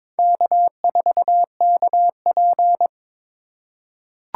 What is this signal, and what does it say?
Morse code, heard as a keyed tone, about 22 wpm, 700 Hz. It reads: K4KP
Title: USA Call Signs - 4 Characters - 4x Character Spacing - 22wpm